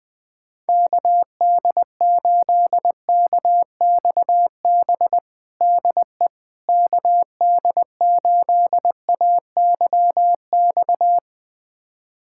Morse code, heard as a keyed tone, about 21 wpm, 700 Hz. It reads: KD8KXB DE KD8AYX
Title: USA Callsign DE Callsign 20wpm